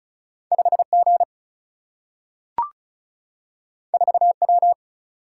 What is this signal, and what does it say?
Morse code, heard as a keyed tone, about 36 wpm, 700 Hz. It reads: 5 G 4 W
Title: Single-Letter Single-Number 35wpm